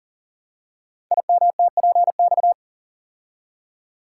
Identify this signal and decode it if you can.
Morse code, heard as a keyed tone, about 40 wpm, 700 Hz. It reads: IMTPX
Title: Code Groups - 5 Letters - 40wpm